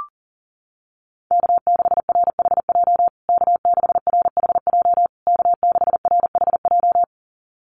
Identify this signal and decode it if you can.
Morse code, heard as a keyed tone, about 40 wpm, 700 Hz. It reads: K6RHJ K6RHJ K6RHJ
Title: USA Call Signs Repeated 3x 40wpm